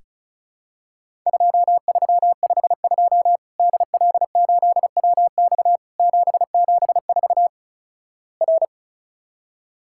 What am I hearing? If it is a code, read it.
Morse code, heard as a keyed tone, about 36 wpm, 700 Hz. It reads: 2352 DL8WX 774 R
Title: Worked All Europe DX Contest - QTC Exchange - 35wpm